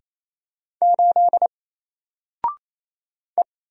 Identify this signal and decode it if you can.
Morse code, heard as a keyed tone, about 28 wpm, 700 Hz. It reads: 8 E
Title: Single Character (Letter:Number:Punctuation:Prosign) Alt 28wpm